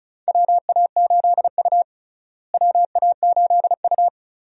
Morse code, heard as a keyed tone, about 34 wpm, 700 Hz. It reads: WA8U WA8U
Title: USA Call Signs - 4 Characters - Repeated 2x - 35wpm